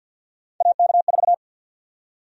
Morse code, heard as a keyed tone, about 51 wpm, 700 Hz. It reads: AK4
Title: USA Call Sign Prefixes - Repeated 2x - 50wpm